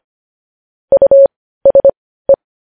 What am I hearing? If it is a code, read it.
Morse code, heard as a keyed tone, about 25 wpm, spaced as for 15 wpm, 550 Hz. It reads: USE
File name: podcast / News headlines-1x-25x15